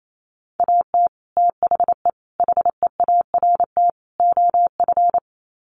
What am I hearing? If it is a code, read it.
Morse code, heard as a keyed tone, about 28 wpm, 700 Hz. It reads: AT THE HEART OF
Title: Sets of 4 Words 28wpm